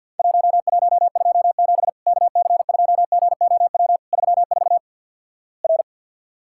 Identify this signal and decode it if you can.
Morse code, heard as a keyed tone, about 53 wpm, 700 Hz. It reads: 1127 KC2GOW 34 R
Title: Worked All Europe DX Contest - QTC Exchange - 50wpm